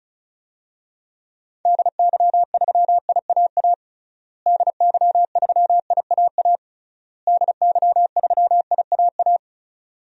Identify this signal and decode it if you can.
Morse code, heard as a keyed tone, about 34 wpm, 700 Hz. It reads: DY3IAA DY3IAA DY3IAA
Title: DXCC-100 Call Signs - 6 Characters - Repeated 3x - 35wpm